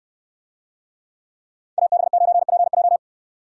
Morse code, heard as a keyed tone, about 68 wpm, 700 Hz. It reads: AB9PJ